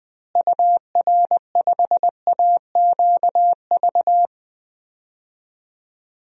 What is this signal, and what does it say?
Morse code, heard as a keyed tone, about 20 wpm, 700 Hz. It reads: UR5AQV
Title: DXCC-100 Call Signs - 6 Characters - Repeated 2x - 20wpm